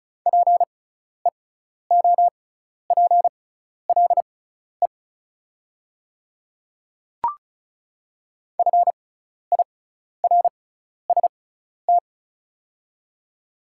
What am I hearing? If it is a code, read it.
Morse code, heard as a keyed tone, about 36 wpm, spaced as for 12 wpm, 700 Hz. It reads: PEOPLE FIRST
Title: 6x Character Spacing - Top 200 Words - 35wpm